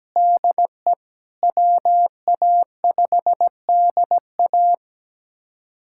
Morse code, heard as a keyed tone, about 17 wpm, 700 Hz. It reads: DE WA5DA
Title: USA Callsign DE Callsign - 5 Characters - 17wpm